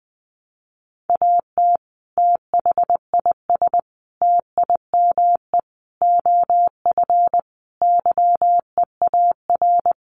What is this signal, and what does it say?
Morse code, heard as a keyed tone, about 20 wpm, 700 Hz. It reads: AT THIS TIME OF YEAR
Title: Sets of 5 Words Encore2 20wpm